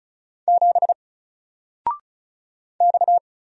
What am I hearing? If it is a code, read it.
Morse code, heard as a keyed tone, about 34 wpm, 700 Hz. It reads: 7 X
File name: Single-Letter-Number-35wpm